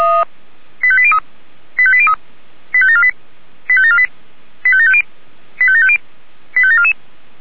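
Five-tone selective call (ZVEI), heard as quick runs of five tones, tones of about 70 ms. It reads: 85803 85803 85748 85748 85750 85750 85744
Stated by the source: Received by SDR; ZVEI five-tone call.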